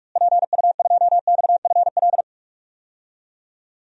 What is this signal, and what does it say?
Morse code, heard as a keyed tone, about 46 wpm, 700 Hz. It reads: PU2XFL
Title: DXCC-100 Call Signs - 6 Characters - Repeated 2x - 45wpm